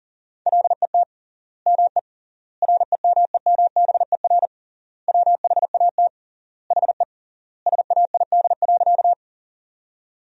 Morse code, heard as a keyed tone, about 37 wpm, 700 Hz. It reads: LET ME REMEMBER WHAT HE SAID.
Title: 3x Word-Spacing - Sentences from Top 400 Words - 40wpm